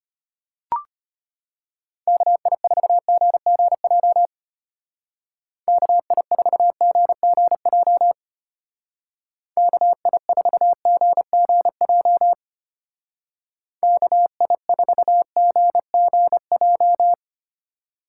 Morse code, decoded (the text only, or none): KI4GGJ KI4GGJ KI4GGJ KI4GGJ